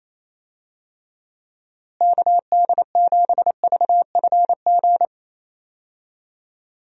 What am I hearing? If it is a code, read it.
Morse code, heard as a keyed tone, about 28 wpm, 700 Hz. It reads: KD7VFG
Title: USA Call Signs - 6 Characters - Repeated 2x - 28wpm